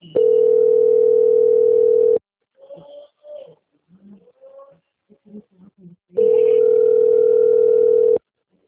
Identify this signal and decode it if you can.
Ringback tone (North America), 440 + 480 Hz.
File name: out- - - - - .9624